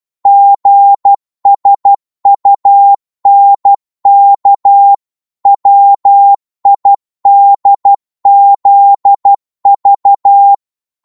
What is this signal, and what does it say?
Morse code, heard as a keyed tone, about 12 wpm, 800 Hz. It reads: GSUNK WIDZV